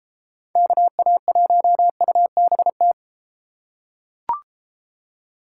Morse code, heard as a keyed tone, about 32 wpm, 700 Hz. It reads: KA1UBT